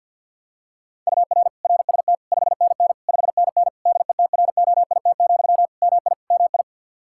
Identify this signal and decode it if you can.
Morse code, heard as a keyed tone, about 49 wpm, 700 Hz. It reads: UR RST 5NN 5NN DETROIT, MI MI
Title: Practice QSO - Exchange 1 - Beginning - 50wpm